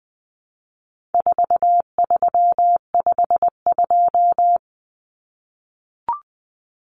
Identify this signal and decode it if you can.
Morse code, heard as a keyed tone, about 20 wpm, 700 Hz. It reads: 4352
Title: Common Cut Numbers - 4-Digit 20wpm